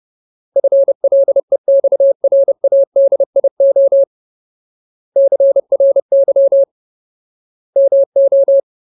Morse code, heard as a keyed tone, about 30 wpm, 550 Hz. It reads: FLEXRADIO CRY MO
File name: Ultimate Word List / 30WPM Ultimate